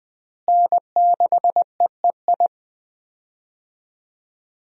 Morse code, heard as a keyed tone, about 20 wpm, 700 Hz. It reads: N6EEI
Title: USA Call Signs - 5 Characters - Repeated 2x - 20wpm